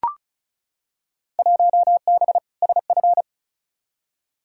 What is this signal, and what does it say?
Morse code, heard as a keyed tone, about 36 wpm, 700 Hz. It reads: 1B SF
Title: Field Day Exchanges Encore - 35wpm